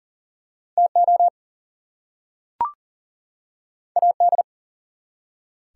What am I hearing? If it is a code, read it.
Morse code, heard as a keyed tone, about 44 wpm, 700 Hz. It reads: TO AD
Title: Two Letter Words - 40wpm